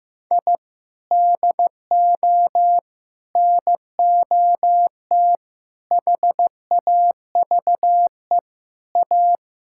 Morse code, heard as a keyed tone, about 15 wpm, 700 Hz. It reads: I DO NOT HAVE A
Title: Sentences from Top 1000 Words - Review Part 5 - 15wpm